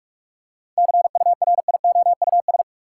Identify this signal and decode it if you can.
Morse code, heard as a keyed tone, about 46 wpm, 700 Hz. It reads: CURIOUS
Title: Seven Letter Words - 45wpm